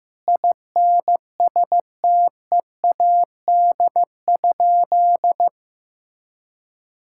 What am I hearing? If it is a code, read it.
Morse code, heard as a keyed tone, about 15 wpm, 700 Hz. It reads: INSTEAD?
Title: Sentences from Top 1000 Words - Review Part 6 - 15wpm